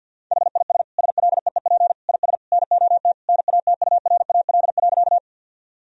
Morse code, heard as a keyed tone, about 50 wpm, 700 Hz. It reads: HIS SLEEP IS NOT NATURAL.
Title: Sentences from Top 1000 Words - Review Part 4 - 50wpm